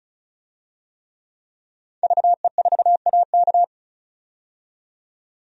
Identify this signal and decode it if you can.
Morse code, heard as a keyed tone, about 36 wpm, 700 Hz. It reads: VE4AK
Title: DXCC-100 Call Signs - 5 Characters - 35wpm